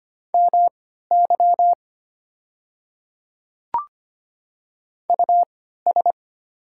Morse code, heard as a keyed tone, about 25 wpm, spaced as for 14 wpm, 690 Hz. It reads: MY US
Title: 3x Character Spacing - Top 200 Words - 25wpm